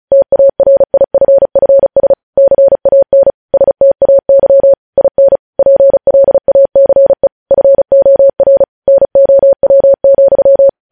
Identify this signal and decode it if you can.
Morse code, heard as a keyed tone, about 36 wpm, 550 Hz. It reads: TARIFFS CAN STAY IN PLACE FOR NOW,
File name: podcast / News headlines-1x-35x35